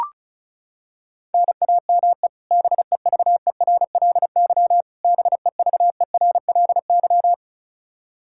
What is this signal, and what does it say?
Morse code, heard as a keyed tone, about 34 wpm, 700 Hz. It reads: NAME BEVERLY BEVERLY
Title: Practice QSO - Exchange 1 - Beginning - 35wpm